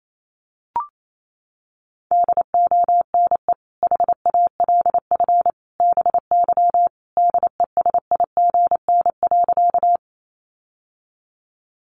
Morse code, heard as a keyed tone, about 28 wpm, 700 Hz. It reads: DONE HALF BY DESIGN.
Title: Sentences from Top 800 Words 28wpm